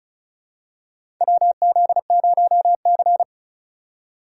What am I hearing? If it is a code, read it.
Morse code, heard as a keyed tone, about 34 wpm, 700 Hz. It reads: WZ0C
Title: USA Call Signs - 4 Characters - 4x Character Spacing - 35wpm